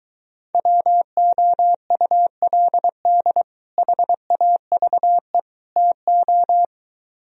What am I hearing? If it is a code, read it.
Morse code, heard as a keyed tone, about 23 wpm, 700 Hz. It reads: WOULD HAVE TO